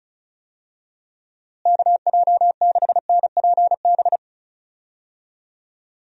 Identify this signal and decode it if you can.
Morse code, heard as a keyed tone, about 34 wpm, 700 Hz. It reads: KJ6NPB
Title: USA Call Signs - 6 Characters - Repeated 3x - 35wpm